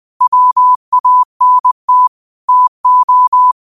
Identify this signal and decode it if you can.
Morse code, heard as a keyed tone, about 20 wpm, 1000 Hz. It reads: WANT TO